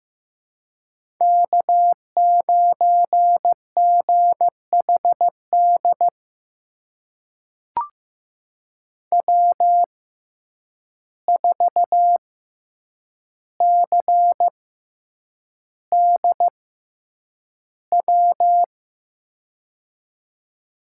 Morse code, heard as a keyed tone, about 16 wpm, spaced as for 8 wpm, 700 Hz. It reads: K9GHD W4CDW
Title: USA Call Signs - 5 Characters - 6x Character Spacing - 15wpm